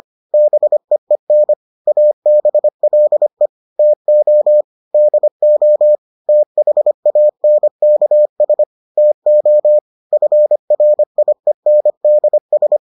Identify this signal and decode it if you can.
Morse code, heard as a keyed tone, about 25 wpm, 600 Hz. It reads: BEEN ABLE TO DO THANKS TO FRIENDS